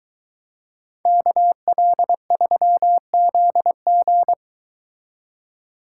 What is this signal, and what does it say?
Morse code, heard as a keyed tone, about 23 wpm, 700 Hz. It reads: KL3ZG